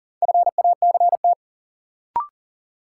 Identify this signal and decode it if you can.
Morse code, heard as a keyed tone, about 40 wpm, 700 Hz. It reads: FACT
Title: Sets of 2 Words - New Word Drill - 40wpm